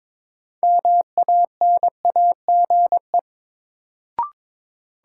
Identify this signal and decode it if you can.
Morse code, heard as a keyed tone, about 22 wpm, 700 Hz. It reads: MANAGE
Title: Sentences from Top 1400 Words - New Word Drill - 22wpm